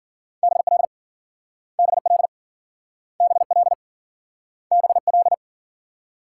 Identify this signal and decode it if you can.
Morse code, heard as a keyed tone, about 55 wpm, 700 Hz. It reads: BL BL BL BL